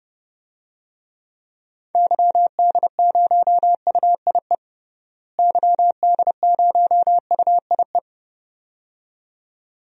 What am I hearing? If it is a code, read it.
Morse code, heard as a keyed tone, about 29 wpm, 700 Hz. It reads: YD0UIE YD0UIE
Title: DXCC-100 Call Signs - 6 Characters - Repeated 2x - 30wpm